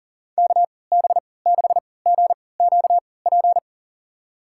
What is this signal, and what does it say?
Morse code, heard as a keyed tone, about 37 wpm, spaced as for 22 wpm, 700 Hz. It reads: KB6GQP
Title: USA Call Signs - 6 Characters - 3x Character Spacing - 40wpm